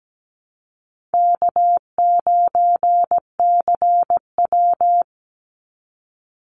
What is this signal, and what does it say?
Morse code, heard as a keyed tone, about 17 wpm, 700 Hz. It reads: K9CW
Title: Easy Call Signs - 17wpm